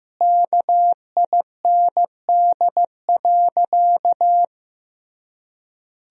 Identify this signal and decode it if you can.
Morse code, heard as a keyed tone, about 15 wpm, 700 Hz. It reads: KIND.
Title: Sentences from Top 500 Words and Abbreviations 15wpm